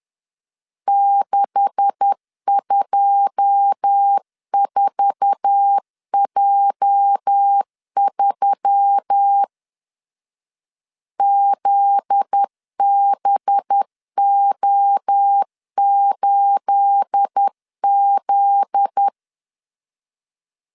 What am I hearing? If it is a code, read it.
Morse code, heard as a keyed tone, about 10 wpm, 800 Hz. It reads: 624J3 ZBO8Z
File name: CW / DET / 1 9 1500C S B 10wpm 4439